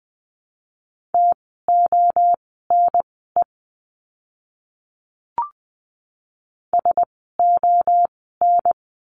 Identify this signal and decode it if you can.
Morse code, heard as a keyed tone, about 20 wpm, spaced as for 15 wpm, 700 Hz. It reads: TONE SON
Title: 068 Lesson 12f - Letter d - Words 20wpm